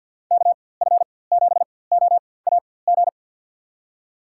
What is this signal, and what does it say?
Morse code, heard as a keyed tone, about 51 wpm, spaced as for 23 wpm, 700 Hz. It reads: KF7OAG
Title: USA Call Signs - 6 Characters - 4x Character Spacing - 50wpm